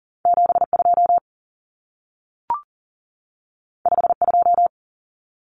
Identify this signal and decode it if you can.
Morse code, heard as a keyed tone, about 40 wpm, 700 Hz. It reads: 72 52
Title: Common Cut Numbers - 2-Digit 40wpm